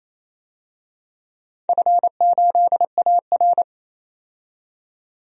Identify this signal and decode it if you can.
Morse code, heard as a keyed tone, about 28 wpm, 700 Hz. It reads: F8AR
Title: DXCC-100 Call Signs - 4 Characters - 28wpm